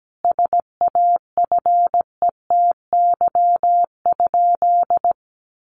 Morse code, heard as a keyed tone, about 17 wpm, 700 Hz. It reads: SAFETY?
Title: Sentences from Top 700 Words 17wpm